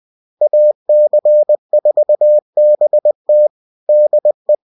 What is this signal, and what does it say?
Morse code, heard as a keyed tone, about 20 wpm, 600 Hz. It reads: AC4BT DE